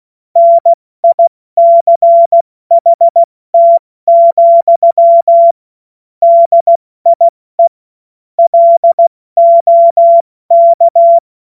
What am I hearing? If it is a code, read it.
Morse code, heard as a keyed tone, about 16 wpm, 680 Hz. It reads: NICHT, DIE LOK
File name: Morsekurs / Witze / Witz 21. Juni 2022 70 F80